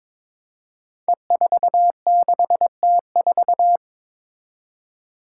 Morse code, heard as a keyed tone, about 22 wpm, 700 Hz. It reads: E46T4
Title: Five Challenging Letter-Number Combinations 22wpm